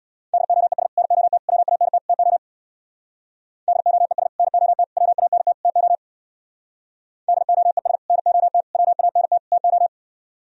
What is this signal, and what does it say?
Morse code, heard as a keyed tone, about 64 wpm, 700 Hz. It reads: DOES NOT WANT TO DOES NOT WANT TO DOES NOT WANT TO